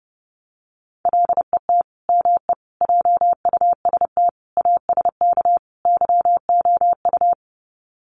Morse code, heard as a keyed tone, about 30 wpm, 700 Hz. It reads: LET ME JUST ASK YOU
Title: Sets of 5 Words Encore2 30wpm